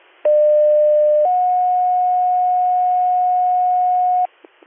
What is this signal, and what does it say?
Two-tone page, heard as a one-second tone, then a three-second tone, about 600 then 730 Hz, with noise. It is heard from 0.2 to 4.3 s.